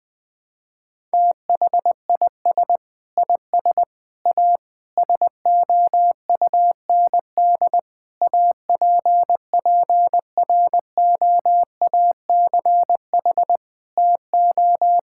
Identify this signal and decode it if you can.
Morse code, heard as a keyed tone, about 21 wpm, 700 Hz. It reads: THIS IS A SOUND APPROACH TO
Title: Sentences from Top 1000 Words - Review Part 6 - 20wpm